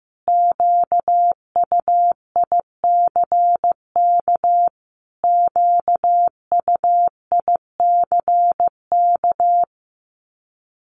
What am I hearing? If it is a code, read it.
Morse code, heard as a keyed tone, about 15 wpm, 700 Hz. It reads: QUICK QUICK
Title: Top 500 Words Repeated 15wpm